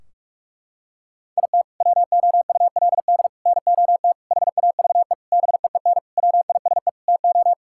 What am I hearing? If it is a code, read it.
Morse code, heard as a keyed tone, about 46 wpm, 700 Hz. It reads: IT WOULD NOT HAVE BEEN WISE TO
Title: Sentences from Top 1600 Words 45wpm